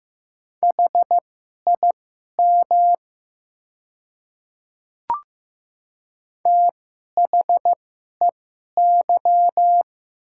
Morse code, heard as a keyed tone, about 15 wpm, spaced as for 11 wpm, 700 Hz. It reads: HIM THEY
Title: 2x Character Spacing - Top 500 Words - 15wpm